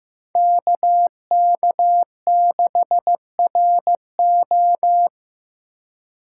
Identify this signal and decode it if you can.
Morse code, heard as a keyed tone, about 15 wpm, 700 Hz. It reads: KK6RO
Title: USA Call Signs - 6x Character Spacing - 15wpm